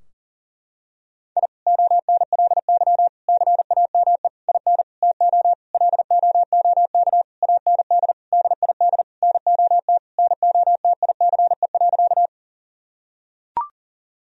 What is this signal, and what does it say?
Morse code, heard as a keyed tone, about 40 wpm, 700 Hz. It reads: I ONLY CAME IN TO LOOK AND DID NOT NOTICE.
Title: Sentences from Top 1000 Words - Review Part 2 - 40wpm